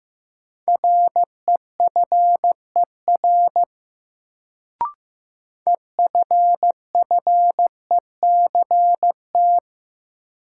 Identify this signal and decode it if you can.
Morse code, heard as a keyed tone, about 15 wpm, 700 Hz. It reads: REFER EFFECT
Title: Sentences from Top 1000 Words - New Word Drill - 15wpm